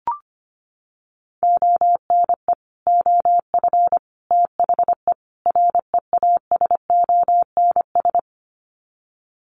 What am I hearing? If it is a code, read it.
Morse code, heard as a keyed tone, about 25 wpm, 700 Hz. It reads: ONE OF THE REASONS
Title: Sets of 4 Words 25wpm